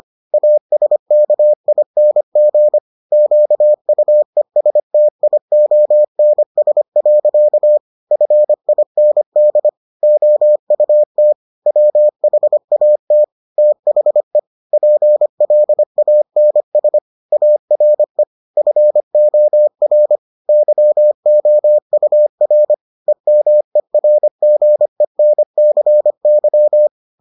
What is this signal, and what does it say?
Morse code, heard as a keyed tone, about 25 wpm, 600 Hz. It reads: ASKING QUESTIONS. FIND OUT WHAT THE PLANS ARE FOR YOUR EMERGENCY